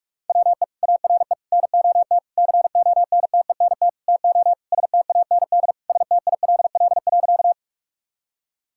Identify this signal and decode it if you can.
Morse code, heard as a keyed tone, about 46 wpm, 700 Hz. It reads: WE ARE NOT CONTENT TO STAND STILL.
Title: Sentences from Top 1000 Words - Review Part 6 - 45wpm